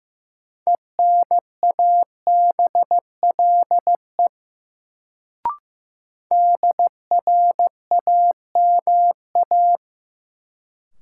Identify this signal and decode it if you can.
Morse code, heard as a keyed tone, about 16 wpm, 700 Hz. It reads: ENABLE DRAMA
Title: Sentences from Top 1500 Words - New Word Drill Alt - 15wpm